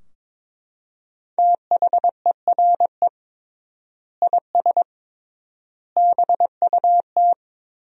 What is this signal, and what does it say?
Morse code, heard as a keyed tone, about 21 wpm, 700 Hz. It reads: THERE IS BUT
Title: 3x Word-Spacing - Sentences from Top 100 Words - 22wpm